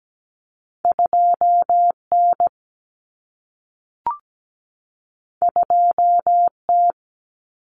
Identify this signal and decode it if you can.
Morse code, heard as a keyed tone, about 17 wpm, 700 Hz. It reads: 2N 2T
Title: Common Cut Numbers - 2-Digit 17wpm